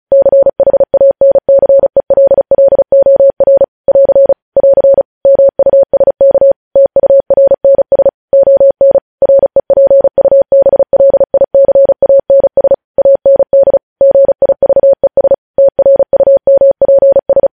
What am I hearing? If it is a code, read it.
Morse code, heard as a keyed tone, about 36 wpm, 550 Hz. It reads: CHANCELLOR <AR> <AR> MUSK TURNS ON REPUBLICANS AND GIVES TRUMPS
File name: podcast / News headlines-1x-35x35